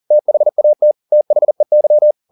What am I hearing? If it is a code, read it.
Morse code, heard as a keyed tone, about 40 wpm, 600 Hz. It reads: THAT THEY